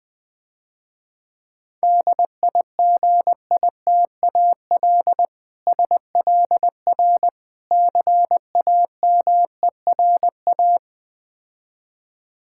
Morse code, heard as a keyed tone, about 20 wpm, 700 Hz. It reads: DIGITAL SLR CAMERA
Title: Difficult Words - Speed-Racing with Hints - 20wpm